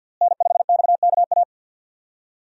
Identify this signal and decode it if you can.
Morse code, heard as a keyed tone, about 53 wpm, 700 Hz. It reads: N5XKA
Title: USA Call Signs - 5 Characters - Repeated 2x - 50wpm